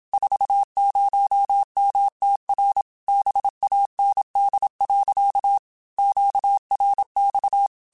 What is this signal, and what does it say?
Morse code, heard as a keyed tone, about 27 wpm, 800 Hz. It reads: 40MTR BAND. QRX
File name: QSOs / QSOs-mp3s / kc7sac@25